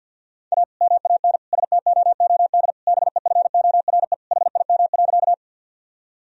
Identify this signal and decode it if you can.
Morse code, heard as a keyed tone, about 51 wpm, 700 Hz. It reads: A MAN STOOD BEFORE HIM.
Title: Sentences from Top 1000 Words - Review Part 2 - 50wpm